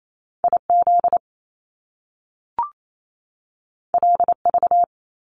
Morse code, heard as a keyed tone, about 28 wpm, 700 Hz. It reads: IZ LV
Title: Top Two Letter Combinations in Words 28wpm